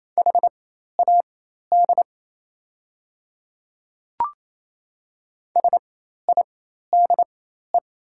Morse code, heard as a keyed tone, about 28 wpm, spaced as for 13 wpm, 700 Hz. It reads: HAD SIDE
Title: 4x Character Spacing - Top 100 Words - 28wpm